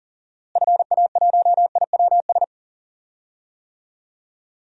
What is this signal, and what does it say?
Morse code, heard as a keyed tone, about 40 wpm, 700 Hz. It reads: FA1IWS
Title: DXCC-100 Call Signs - 6 Characters - Repeated 3x - 40wpm